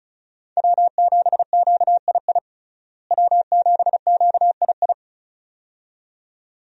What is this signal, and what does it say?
Morse code, heard as a keyed tone, about 34 wpm, 700 Hz. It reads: W7QII W7QII
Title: USA Call Signs - 5 Characters - Repeated 2x - 35wpm